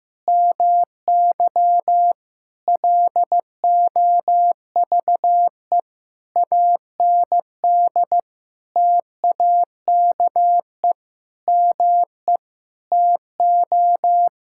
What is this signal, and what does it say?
Morse code, heard as a keyed tone, about 15 wpm, 700 Hz. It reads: MY LOVE AND TAKE ME TO
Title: Sentences from Top 1000 Words - Review Part 6 - 15wpm